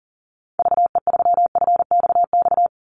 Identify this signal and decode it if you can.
Morse code, heard as a keyed tone, about 40 wpm, 700 Hz. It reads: VE3FXX
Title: Call Signs - 40wpm